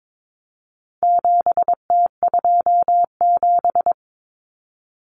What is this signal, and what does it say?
Morse code, heard as a keyed tone, about 22 wpm, 700 Hz. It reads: 7T27
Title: Common Cut Numbers - 4-Digit 22wpm